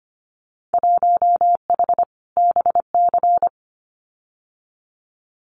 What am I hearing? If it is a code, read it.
Morse code, heard as a keyed tone, about 25 wpm, 690 Hz. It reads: 1H BC
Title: Winter Field Day Exchanges 25wpm